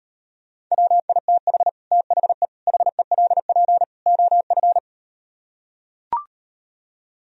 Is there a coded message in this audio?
Yes: Morse code from 0.7 to 4.8 s, heard as a keyed tone, about 38 wpm, 700 Hz. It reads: WITH THE HELP OF